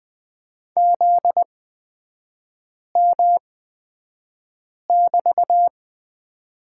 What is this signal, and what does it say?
Morse code, heard as a keyed tone, about 20 wpm, spaced as for 5 wpm, 700 Hz. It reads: ZM<BT>
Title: 243 Lesson 39e - Letter z - Three Characters 20wpm